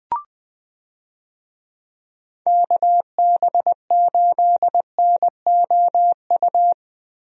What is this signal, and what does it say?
Morse code, heard as a keyed tone, about 21 wpm, 700 Hz. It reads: KB8NOU
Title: USA Call Signs - 6 Characters - Repeated 2x - 20wpm